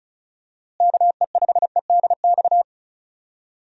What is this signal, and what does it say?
Morse code, heard as a keyed tone, about 36 wpm, 700 Hz. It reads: KE5EDX
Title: USA Call Signs - 6 Characters - Repeated 2x - 35wpm